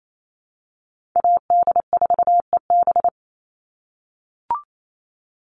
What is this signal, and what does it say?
Morse code, heard as a keyed tone, about 28 wpm, 700 Hz. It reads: AD4EB
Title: Call Signs - 28wpm